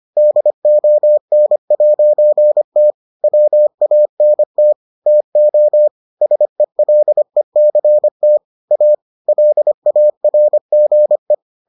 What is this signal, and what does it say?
Morse code, heard as a keyed tone, about 25 wpm, 600 Hz. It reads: DON'T WANT TO SELECT A LARGE